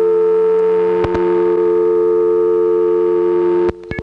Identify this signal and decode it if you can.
Dial tone (North America), 350 + 440 Hz.